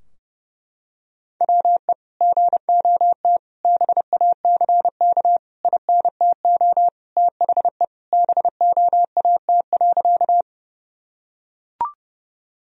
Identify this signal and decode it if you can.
Morse code, heard as a keyed tone, about 32 wpm, 700 Hz. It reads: WE GOT BACK INTO THE BOAT.
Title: Sentences from Top 1000 Words - Review Part 3 - 30wpm